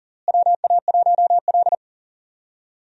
Morse code, heard as a keyed tone, about 40 wpm, 700 Hz. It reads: WA1L
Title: USA Call Signs - 4 Characters - Repeated 2x - 40wpm